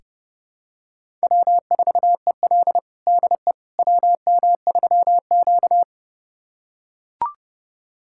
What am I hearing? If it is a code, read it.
Morse code, heard as a keyed tone, about 32 wpm, 700 Hz. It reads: W4EL DE WM3Q
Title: USA Callsign DE Callsign - 4 Characters - 30wpm